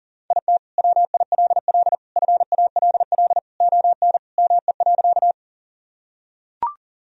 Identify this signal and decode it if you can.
Morse code, heard as a keyed tone, about 40 wpm, 700 Hz. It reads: IT WILL FALL ON ME.
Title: Sentences from Top 1000 Words - Review Part 2 - 40wpm